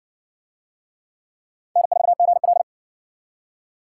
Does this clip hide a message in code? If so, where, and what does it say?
Yes: Morse code from 1.8 to 2.6 s, heard as a keyed tone, about 60 wpm, 700 Hz. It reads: N4GL